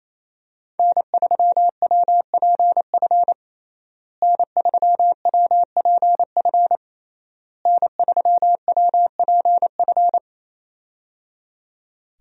Morse code, heard as a keyed tone, about 28 wpm, 700 Hz. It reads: N3WPF N3WPF N3WPF
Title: USA Call Signs - 5 Characters - Repeated 3x - 28wpm